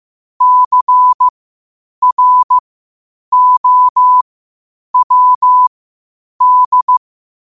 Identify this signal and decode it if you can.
Morse code, heard as a keyed tone, about 15 wpm, spaced as for 8 wpm, 1000 Hz. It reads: CROWD
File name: qotdfiles / 8wpm